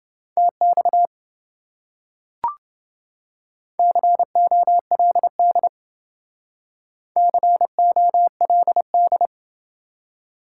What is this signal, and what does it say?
Morse code, heard as a keyed tone, about 28 wpm, 700 Hz. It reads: TX COLD COLD